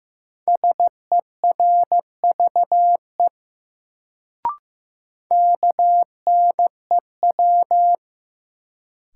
Morse code, heard as a keyed tone, about 15 wpm, 700 Hz. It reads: SERVE KNEW
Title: Sentences from Top 400 Words - New Word Drill - 15wpm